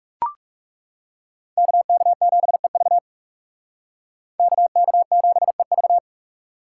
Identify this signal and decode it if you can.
Morse code, heard as a keyed tone, about 37 wpm, 700 Hz. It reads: KK7EV KK7EV
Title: Speed Racing - USA 2x2 Call Signs 30wpm